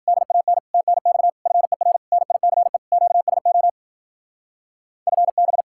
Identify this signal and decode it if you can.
Morse code, heard as a keyed tone, about 52 wpm, 700 Hz. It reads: DAN TNX FER NICE QSO FB